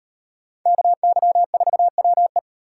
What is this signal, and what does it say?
Morse code, heard as a keyed tone, about 38 wpm, 700 Hz. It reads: KY4WE